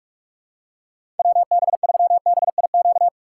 Morse code, heard as a keyed tone, about 46 wpm, 700 Hz. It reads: WB3BIQ
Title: K1USN SST QSO Practice - 45wpm